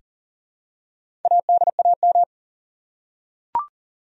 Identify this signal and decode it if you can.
Morse code, heard as a keyed tone, about 40 wpm, 700 Hz. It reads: ADAM
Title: USA Common Names - Max 4 Letters - 40wpm